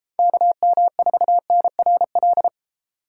Morse code, heard as a keyed tone, about 33 wpm, 700 Hz. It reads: KM4NRL